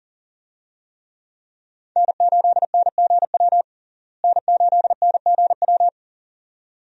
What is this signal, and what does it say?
Morse code, heard as a keyed tone, about 40 wpm, 700 Hz. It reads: N8NGW N8NGW
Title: USA Call Signs - 5 Characters - Repeated 2x - 40wpm